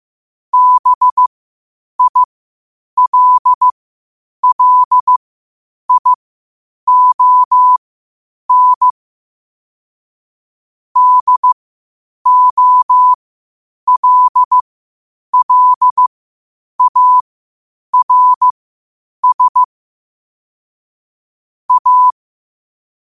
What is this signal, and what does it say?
Morse code, heard as a keyed tone, about 15 wpm, spaced as for 8 wpm, 1000 Hz. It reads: BILLION DOLLARS A
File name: qotdfiles / 8wpm